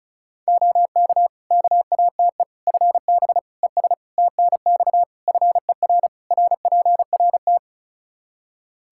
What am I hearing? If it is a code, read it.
Morse code, heard as a keyed tone, about 34 wpm, 700 Hz. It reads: OK KATE FB ES TNX FER RPRT
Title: Practice QSO - Exchange 2 - Info - 35wpm